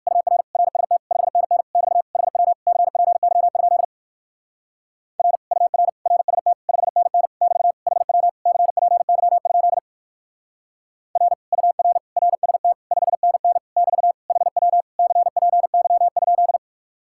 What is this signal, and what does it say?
Morse code, heard as a keyed tone, about 62 wpm, 700 Hz. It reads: UR RST 5NN <BT> HW CPY? R UR RST 5NN <BT> HW CPY? R UR RST 5NN <BT> HW CPY?